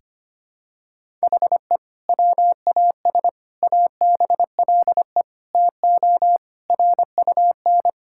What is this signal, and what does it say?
Morse code, heard as a keyed tone, about 25 wpm, 700 Hz. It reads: HE WAS ABLE TO RUN
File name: practice / 25wpm / Sentences with 2-Word Phrases - 25wpm